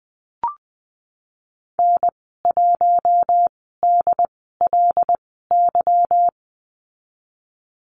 Morse code, heard as a keyed tone, about 20 wpm, spaced as for 14 wpm, 700 Hz. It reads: N1DLY
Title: 186 Lesson 30g - Letter b - Call Signs 20wpm